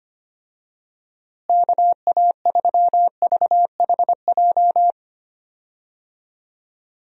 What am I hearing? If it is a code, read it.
Morse code, heard as a keyed tone, about 25 wpm, 700 Hz. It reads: KA3VHJ
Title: USA Call Signs - 6 Characters - Repeated 2x - 25wpm